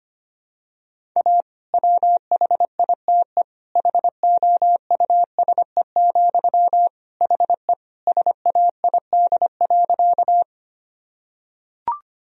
Morse code, heard as a keyed tone, about 25 wpm, 700 Hz. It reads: A WHITE HOUSE, HE SAID.
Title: Sentences from Top 1000 Words - Review Part 1 - 25wpm